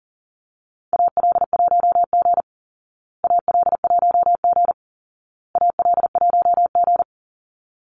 Morse code, heard as a keyed tone, about 40 wpm, 700 Hz. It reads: AL1G AL1G AL1G
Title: Call Signs Repeated 2x - 40wpm